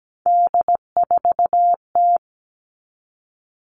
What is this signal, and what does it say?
Morse code, heard as a keyed tone, about 17 wpm, 700 Hz. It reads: D4T
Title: All Cut Numbers - 3-Digit 17wpm